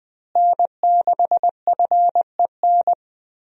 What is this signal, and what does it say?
Morse code, heard as a keyed tone, about 20 wpm, 700 Hz. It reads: N6FEN
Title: USA Call Signs - 5 Characters - Repeated 3x - 20wpm